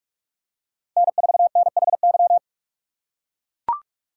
Morse code, heard as a keyed tone, about 44 wpm, 700 Hz. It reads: N4NHY